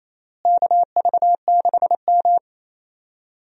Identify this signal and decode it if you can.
Morse code, heard as a keyed tone, about 28 wpm, 700 Hz. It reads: KV6M
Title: USA Call Signs - 4 Characters - Repeated 3x - 28wpm